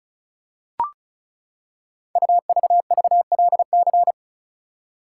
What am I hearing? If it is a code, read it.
Morse code, heard as a keyed tone, about 36 wpm, 690 Hz. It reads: UVVLC
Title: Code Groups - 5 Letters - 35wpm